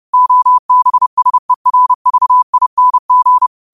Morse code, heard as a keyed tone, about 30 wpm, 1000 Hz. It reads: OBSERVING